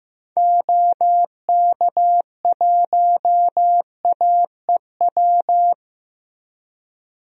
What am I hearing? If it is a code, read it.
Morse code, heard as a keyed tone, about 15 wpm, 700 Hz. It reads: OK1AEW
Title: DXCC-100 Call Signs - 6 Characters - 15wpm